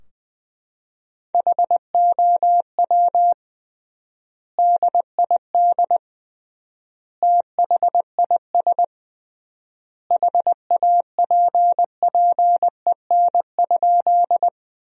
Morse code, heard as a keyed tone, about 20 wpm, 700 Hz. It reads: HOW DID THIS HAPPEN?
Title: 3x Word-Spacing - Sentences from Top 400 Words - 20wpm